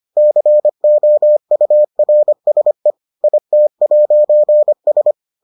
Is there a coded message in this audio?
Yes: Morse code from 0.2 to 5.1 s, heard as a keyed tone, about 25 wpm, 600 Hz. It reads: COURSE IT'S